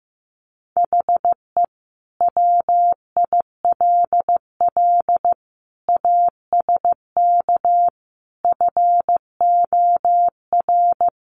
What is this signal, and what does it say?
Morse code, heard as a keyed tone, about 15 wpm, 700 Hz. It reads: HE WILL ASK FOR
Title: Sentences from Top 200 Words 15wpm